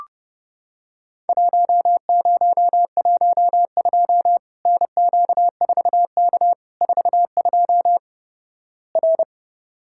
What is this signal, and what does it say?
Morse code, heard as a keyed tone, about 32 wpm, 700 Hz. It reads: 1012 NQ4K 42 R
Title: Worked All Europe DX Contest - QTC Exchange - 30wpm